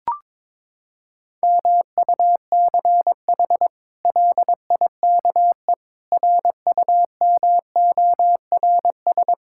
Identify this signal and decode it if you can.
Morse code, heard as a keyed tone, about 22 wpm, 700 Hz. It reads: MUCH LIKE RUMORS
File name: practice / 22wpm / Sets of 8 Words - 22wpm